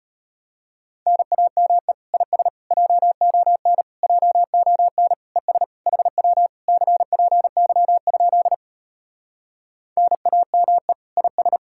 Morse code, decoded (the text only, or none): NAME IS JON JON ES HW CPY? NAME IS